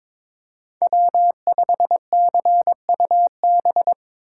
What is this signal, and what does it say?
Morse code, heard as a keyed tone, about 22 wpm, 700 Hz. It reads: W5CUB
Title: USA Call Signs - 5 Characters - 6x Character Spacing - 22wpm